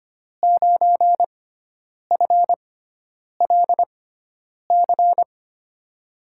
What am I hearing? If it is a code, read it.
Morse code, heard as a keyed tone, about 25 wpm, spaced as for 9 wpm, 700 Hz. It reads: 9FLC
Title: USA Call Signs - 5 Characters - 6x Character Spacing - 25wpm